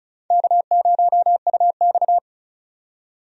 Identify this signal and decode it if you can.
Morse code, heard as a keyed tone, about 34 wpm, 700 Hz. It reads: K0UX
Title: USA Call Signs - 4 Characters - 35wpm